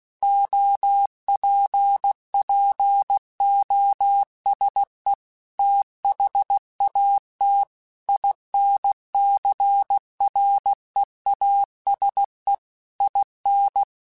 Morse code, heard as a keyed tone, about 16 wpm, 800 Hz. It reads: OPPOSE THAT INCREASE IN